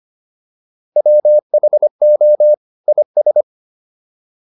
Morse code, heard as a keyed tone, about 25 wpm, 600 Hz. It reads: WHO IS